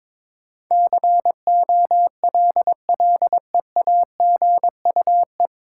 Morse code, heard as a keyed tone, about 22 wpm, 700 Hz. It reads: COLLEAGUE
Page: Morse Code Ninja practice 22wpm